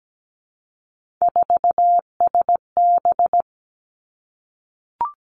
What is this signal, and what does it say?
Morse code, heard as a keyed tone, about 17 wpm, 700 Hz. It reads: 4SB
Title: Three Challenging Letter-Number Combinations 17wpm